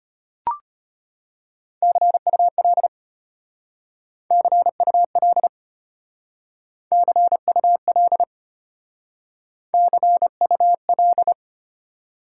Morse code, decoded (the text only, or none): CUL CUL CUL CUL